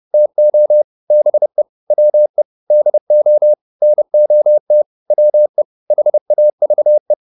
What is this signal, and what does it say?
Morse code, heard as a keyed tone, about 30 wpm, 600 Hz. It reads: TO BE WE DO NOT WE HAVE